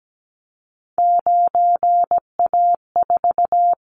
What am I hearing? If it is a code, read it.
Morse code, heard as a keyed tone, about 17 wpm, 700 Hz. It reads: 9A4
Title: Callsign Prefixes for Top 25 DXCC - 17wpm